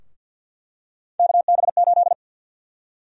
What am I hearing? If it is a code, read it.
Morse code, heard as a keyed tone, about 46 wpm, 700 Hz. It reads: KB8
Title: USA Call Sign Prefixes - 50wpm